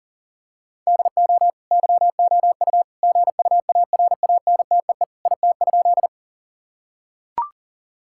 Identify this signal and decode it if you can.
Morse code, heard as a keyed tone, about 37 wpm, 700 Hz. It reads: DO YOU GUARANTEE IT?
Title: Sentences from Top 1600 Words 40wpm